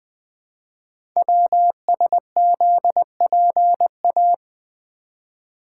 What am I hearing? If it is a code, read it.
Morse code, heard as a keyed tone, about 20 wpm, 700 Hz. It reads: WSZPA
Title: Code Groups - 5 Letters - 20wpm